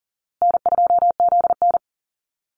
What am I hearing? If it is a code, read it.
Morse code, heard as a keyed tone, about 40 wpm, 700 Hz. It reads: N2ZN
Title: Call Signs Repeated 2x - 40wpm